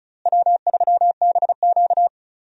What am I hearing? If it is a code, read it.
Morse code, heard as a keyed tone, about 36 wpm, 700 Hz. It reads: W3BQ
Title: USA Call Signs - 4 Characters - Repeated 2x - 35wpm